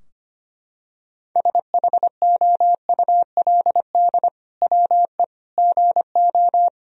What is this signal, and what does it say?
Morse code, heard as a keyed tone, about 25 wpm, 700 Hz. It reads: SHOULD WE GO
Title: Sentences from Top 1000 Words - Review Part 3 - 25wpm